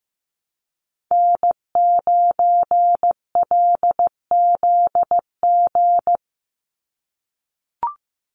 Morse code, heard as a keyed tone, about 15 wpm, 700 Hz. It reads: N9LZG
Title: USA Call Signs 15wpm